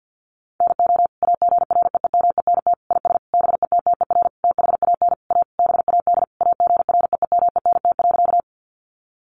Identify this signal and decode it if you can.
Morse code, heard as a keyed tone, about 50 wpm, 700 Hz. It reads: NO AGREEMENT IS BETTER THAN A BAD AGREEMENT.
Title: Sentences from Top 1500 Words 50wpm